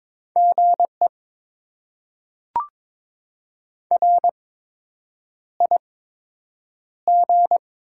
Morse code, heard as a keyed tone, about 22 wpm, spaced as for 8 wpm, 700 Hz. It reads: GE RIG
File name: Top 20 QSO Elements - 8x Character Spacing - 22wpm